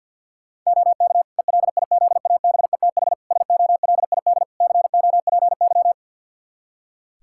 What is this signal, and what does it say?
Morse code, heard as a keyed tone, about 52 wpm, 700 Hz. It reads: OK ELIZABETH SOLID COPY
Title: Practice QSO - Exchange 3 - More Info - 50wpm